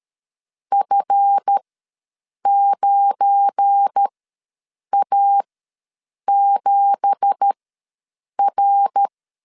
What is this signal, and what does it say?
Morse code, heard as a keyed tone, about 13 wpm, spaced as for 7 wpm, 800 Hz. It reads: F9A7R
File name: LEZIONI / 10 SIGNS / 10B INT ws 12wpm 3527